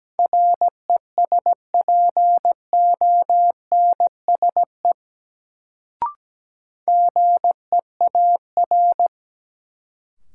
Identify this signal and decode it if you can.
Morse code, heard as a keyed tone, about 17 wpm, 700 Hz. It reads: RESPONSE GEAR
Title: Sentences from Top 1400 Words - New Word Drill Alt - 17wpm